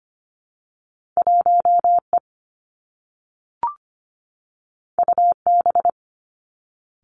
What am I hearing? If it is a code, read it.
Morse code, heard as a keyed tone, about 25 wpm, 700 Hz. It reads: 1E UB
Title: Two Challenging Letter-Number Combinations 25wpm